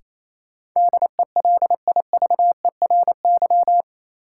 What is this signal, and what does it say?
Morse code, heard as a keyed tone, about 28 wpm, 700 Hz. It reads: DELIVERY
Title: Sentences from Top 1500 Words - New Word Drill Alt - 28wpm